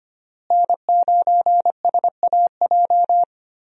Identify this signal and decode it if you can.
Morse code, heard as a keyed tone, about 25 wpm, 700 Hz. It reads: N9SAJ
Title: USA Call Signs - 5 Characters - Repeated 3x - 25wpm